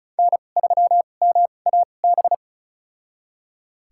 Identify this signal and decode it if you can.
Morse code, heard as a keyed tone, about 36 wpm, spaced as for 25 wpm, 700 Hz. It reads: N3MAB
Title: USA Call Signs - 5 Characters - 2x Character Spacing - 35wpm